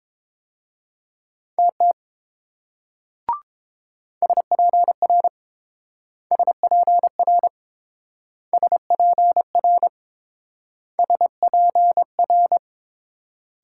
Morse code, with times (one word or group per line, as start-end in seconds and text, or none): 1.6-1.9 TT
4.2-5.3 SPR
6.3-7.5 SPR
8.5-9.9 SPR
11.0-12.6 SPR